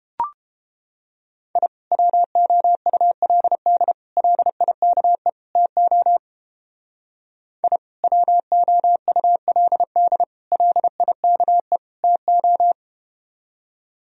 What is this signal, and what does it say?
Morse code, heard as a keyed tone, about 30 wpm, 700 Hz. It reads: I WOULD LIKE TO I WOULD LIKE TO